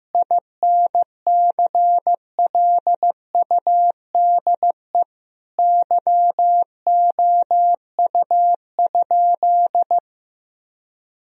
Morse code, heard as a keyed tone, about 16 wpm, 700 Hz. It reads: INCLUDE YOU?
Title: Sentences from Top 1000 Words - Review Part 3 - 15wpm